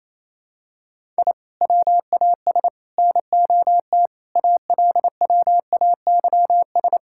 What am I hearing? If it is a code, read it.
Morse code, heard as a keyed tone, about 28 wpm, 700 Hz. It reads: I WAS NOT ALWAYS
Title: Sentences from Top 1000 Words - Review Part 6 - 28wpm